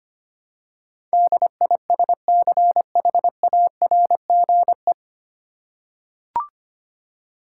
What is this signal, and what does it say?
Morse code, heard as a keyed tone, about 25 wpm, 700 Hz. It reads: DISCHARGE
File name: practice / 25wpm / 9 Letter Words - First 3 Letters Given - 25wpm